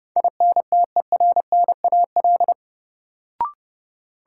Morse code, heard as a keyed tone, about 30 wpm, 700 Hz. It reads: INTERNAL
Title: Sentences from Top 1400 Words - New Word Drill - 30wpm